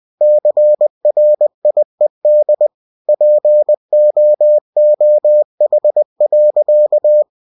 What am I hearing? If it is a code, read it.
Morse code, heard as a keyed tone, about 20 wpm, 600 Hz. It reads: CRIED POOH.
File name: cw-books / winnie-the-pooh / winnie-20wpm-page-0017